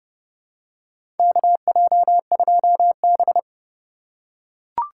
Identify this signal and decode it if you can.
Morse code, heard as a keyed tone, about 30 wpm, 700 Hz. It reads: KJ2B
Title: USA Call Signs - 3x Character Spacing - 30wpm